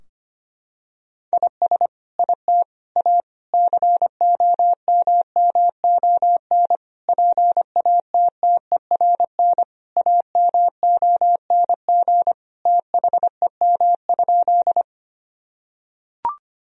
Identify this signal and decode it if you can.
Morse code, heard as a keyed tone, about 25 wpm, 700 Hz. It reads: IS IT A COMMON PATTERN AMONG THEM?
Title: Sentences from Top 1000 Words - Review Part 2 - 25wpm